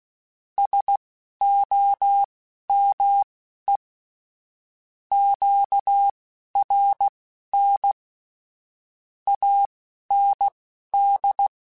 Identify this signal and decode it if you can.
Morse code, heard as a keyed tone, about 16 wpm, spaced as for 11 wpm, 800 Hz. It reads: SOME QRN AND
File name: (spaced) / QSOs Spaced-mp3s / WD0BGP 1 @15 sp